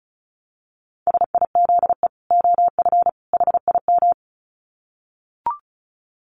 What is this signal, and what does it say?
Morse code, heard as a keyed tone, about 34 wpm, 700 Hz. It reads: SIZE OF HIM
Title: Dit Challenge 35wpm